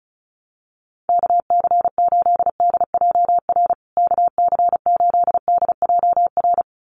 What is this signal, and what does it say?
Morse code, heard as a keyed tone, about 34 wpm, 700 Hz. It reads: KC8DJR KC8DJR
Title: USA Call Signs Repeated 2x 35wpm